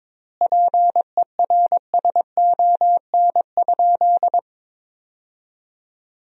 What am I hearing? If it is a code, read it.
Morse code, heard as a keyed tone, about 22 wpm, 700 Hz. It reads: PERSON?
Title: 4x Word-Spacing - Sentences from Top 400 Words - 22wpm